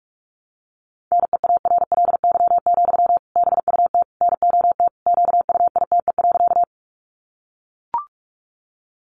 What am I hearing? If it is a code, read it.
Morse code, heard as a keyed tone, about 44 wpm, 700 Hz. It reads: NEARLY, BUT NOT QUITE.
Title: Sentences from Top 1300 Words 45wpm